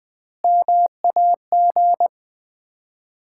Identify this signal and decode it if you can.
Morse code, heard as a keyed tone, about 20 wpm, 700 Hz. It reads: MAG
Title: Common QSO Phrases - 4x Word-Spacing - 20wpm